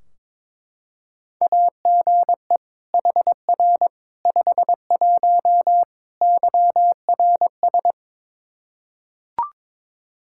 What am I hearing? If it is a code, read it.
Morse code, heard as a keyed tone, about 21 wpm, 700 Hz. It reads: AGE HR 51 YRS
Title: Practice QSO - Exchange 3 - More Info - 22wpm